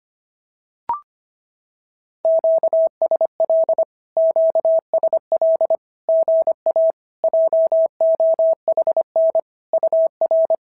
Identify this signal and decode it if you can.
Morse code, heard as a keyed tone, about 26 wpm, 650 Hz. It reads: QSL QSL GA JOHN UR
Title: SOTA QSO Practice 25wpm